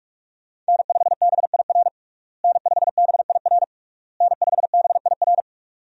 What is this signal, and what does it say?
Morse code, heard as a keyed tone, about 42 wpm, 700 Hz. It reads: N5BIR N5BIR N5BIR
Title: USA Call Signs - 5 Characters - Repeated 3x - 45wpm